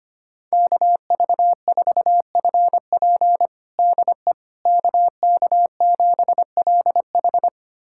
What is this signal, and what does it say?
Morse code, heard as a keyed tone, about 25 wpm, 700 Hz. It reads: KV4FP DE KK7LH
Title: USA Callsign DE Callsign - 5 Characters - 25wpm